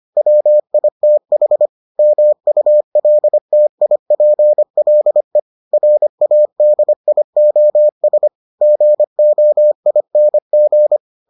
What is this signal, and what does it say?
Morse code, heard as a keyed tone, about 25 wpm, 600 Hz. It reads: WITH MULTIPLE RADIOS GOING